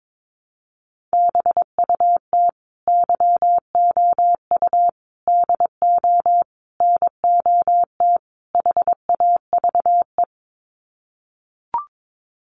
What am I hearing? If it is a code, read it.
Morse code, heard as a keyed tone, about 22 wpm, 700 Hz. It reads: BUT YOU DO NOT HAVE
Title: Sets of 5 Words Encore 22wpm